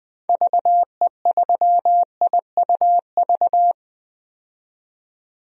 Morse code, heard as a keyed tone, about 20 wpm, 700 Hz. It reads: VE3IUV
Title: DXCC-100 Call Signs - 6 Characters - 20wpm